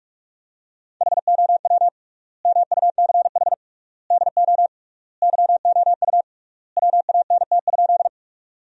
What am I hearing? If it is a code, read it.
Morse code, heard as a keyed tone, about 46 wpm, 700 Hz. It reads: HOW MUCH DO YOU WANT?
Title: 3x Word-Spacing - Sentences from Top 200 Words - 45wpm